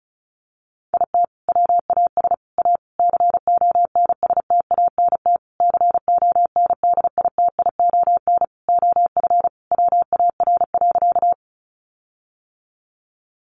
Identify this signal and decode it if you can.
Morse code, heard as a keyed tone, about 34 wpm, 700 Hz. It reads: IT WAS A CONSTANT CONDITION OF WAR.
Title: Sentences from Top 1500 Words 35wpm